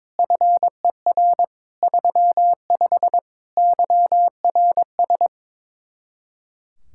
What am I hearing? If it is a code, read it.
Morse code, heard as a keyed tone, about 22 wpm, 700 Hz. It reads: FER 35 YRS
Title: Speed-Racing Common QSO Phrases - 22wpm